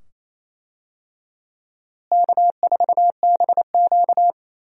Morse code, heard as a keyed tone, about 28 wpm, 700 Hz. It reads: K4BQ
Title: USA Call Signs - 4 Characters - Repeated 2x - 28wpm